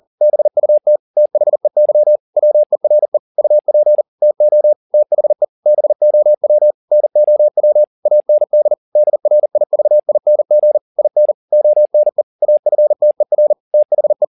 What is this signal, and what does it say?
Morse code, heard as a keyed tone, about 40 wpm, 600 Hz. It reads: BUT THEY WERE UP TO THE BOW NOW AND DRIVING IN ONE AFTER THE
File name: the old man and the sea 2 40wpm